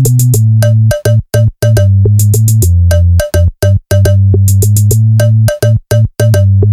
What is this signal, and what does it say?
There is no code.